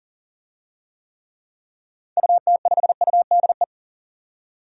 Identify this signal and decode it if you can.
Morse code, heard as a keyed tone, about 40 wpm, 700 Hz. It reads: UT5UDE
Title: DXCC-100 Call Signs - 6 Characters - 40wpm